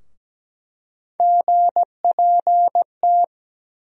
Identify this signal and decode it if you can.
Morse code, heard as a keyed tone, about 17 wpm, 700 Hz. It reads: GPT
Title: USA Call Sign Postfixes - 3 Letters 17wpm